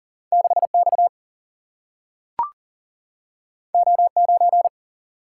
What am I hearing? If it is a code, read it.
Morse code, heard as a keyed tone, about 40 wpm, 700 Hz. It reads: 6X O9
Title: Single-Letter Single-Number 40wpm